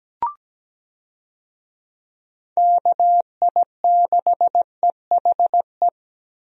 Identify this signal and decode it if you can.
Morse code, heard as a keyed tone, about 17 wpm, 700 Hz. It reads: KI6EHE
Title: USA Call Signs - 6 Characters - 17wpm